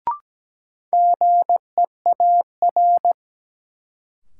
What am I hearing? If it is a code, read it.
Morse code, heard as a keyed tone, about 17 wpm, 700 Hz. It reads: GEAR
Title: Sentences from Top 1400 Words - New Word Drill Alt - 17wpm